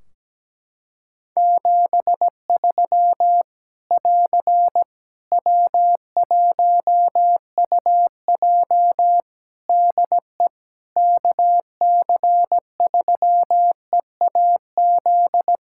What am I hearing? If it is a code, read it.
Morse code, heard as a keyed tone, about 17 wpm, 700 Hz. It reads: 73 <AR> W1UJ DE KC3EAZ
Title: Practice QSO - Exchange 4 - Ending - 17wpm